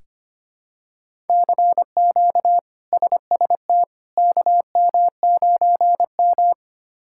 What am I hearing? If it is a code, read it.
Morse code, heard as a keyed tone, about 25 wpm, 700 Hz. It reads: CQ SST KM9M
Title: K1USN SST QSO Practice - 25wpm